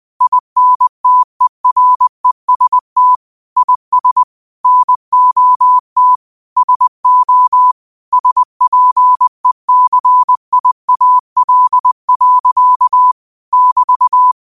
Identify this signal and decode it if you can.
Morse code, heard as a keyed tone, about 20 wpm, 1000 Hz. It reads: INTEREST IS NOT SO SPECIAL. <BT>
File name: 20wpm